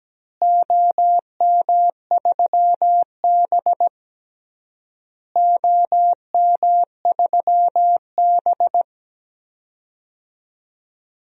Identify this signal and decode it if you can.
Morse code, heard as a keyed tone, about 17 wpm, 700 Hz. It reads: OM3B OM3B
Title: DXCC-100 Call Signs - 4 Characters - Repeated 3x - 17wpm